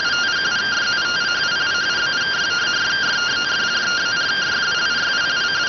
Two-tone teleprinter signal, about 170 Hz shift.
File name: findings / CT AXIAL -256.gen-13 rx